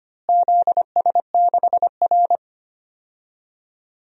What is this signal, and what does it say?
Morse code, heard as a keyed tone, about 25 wpm, 700 Hz. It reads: ZS6R
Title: DXCC-100 Call Signs - 4 Characters - Repeated 2x - 25wpm